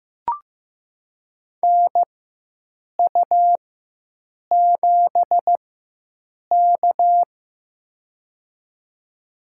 Morse code, heard as a keyed tone, about 15 wpm, spaced as for 7 wpm, 700 Hz. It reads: NU7K
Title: USA Call Signs - 4 Characters - 4x Character Spacing - 15wpm